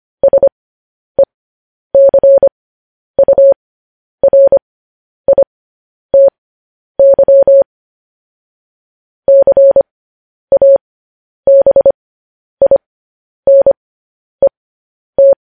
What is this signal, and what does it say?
Morse code, heard as a keyed tone, about 25 wpm, spaced as for 10 wpm, 550 Hz. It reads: SECURITY CABINET
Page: News Headlines in Morse Code at 10 WPM